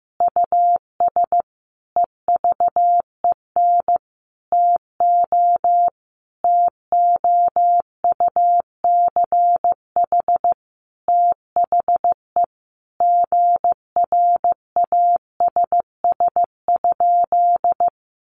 Morse code, heard as a keyed tone, about 15 wpm, 700 Hz. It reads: US EVEN TO TOUCH THE GRASS?
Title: A Sound Of Thunder 15wpm